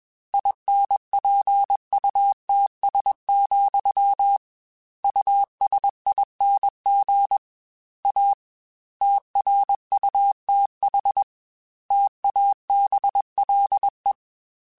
Morse code, heard as a keyed tone, about 21 wpm, 800 Hz. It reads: INPUTS, USING A TRUTH TABLE
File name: Technical Subjects / Technical Subjects-mp3s / Gates Intro @20